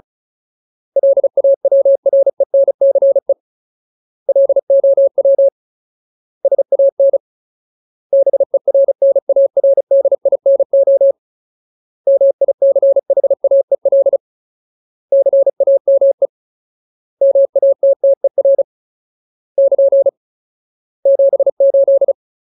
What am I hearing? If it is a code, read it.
Morse code, heard as a keyed tone, about 34 wpm, 550 Hz. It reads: LAWRENCE LOW SAN BERNARDINO MICHAEL CAME MATTER <KN> 78
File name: Ultimate Word List / 35WPM Ultimate